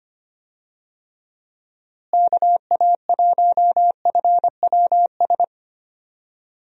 Morse code, heard as a keyed tone, about 25 wpm, 700 Hz. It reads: KA1FWS
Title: USA Call Signs - 6 Characters - Repeated 2x - 25wpm